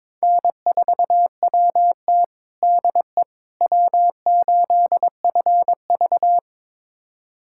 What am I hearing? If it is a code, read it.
Morse code, heard as a keyed tone, about 22 wpm, 700 Hz. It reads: N4WT DE W8FV
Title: USA Callsign DE Callsign - 4 Characters - 22wpm